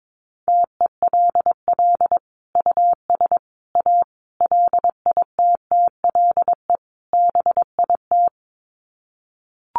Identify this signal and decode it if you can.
Morse code, heard as a keyed tone, about 22 wpm, 700 Hz. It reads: TELL US A LITTLE BIT
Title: Sets of 5 Words Encore2 22wpm